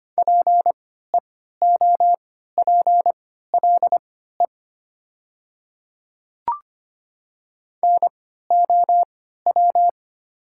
Morse code, heard as a keyed tone, about 25 wpm, spaced as for 14 wpm, 700 Hz. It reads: PEOPLE NOW
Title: 3x Character Spacing - Top 100 Words - 25wpm